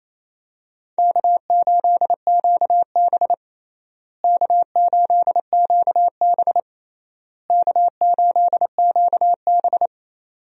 Morse code, heard as a keyed tone, about 28 wpm, 700 Hz. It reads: K8QB K8QB K8QB
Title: USA Call Signs - 4 Characters - Repeated 3x - 28wpm